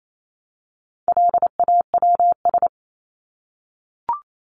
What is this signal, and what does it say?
Morse code, heard as a keyed tone, about 28 wpm, 700 Hz. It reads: LAWS
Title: ES:S Suffix - Top 100 Words with Base Words 28wpm